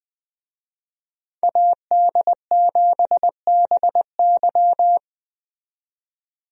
Morse code, heard as a keyed tone, about 20 wpm, 700 Hz. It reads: AD7BY
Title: USA Call Signs - 5 Characters - 20wpm